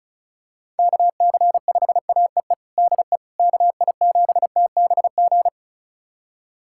Morse code, heard as a keyed tone, about 36 wpm, 700 Hz. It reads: KC5AEE DE KI7TBG
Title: USA Callsign DE Callsign - 6 Characters - 35wpm